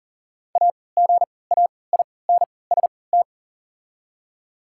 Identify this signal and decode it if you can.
Morse code, heard as a keyed tone, about 40 wpm, spaced as for 23 wpm, 700 Hz. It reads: AGAINST
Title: 3x Character Spacing - Top 500 Words - 40wpm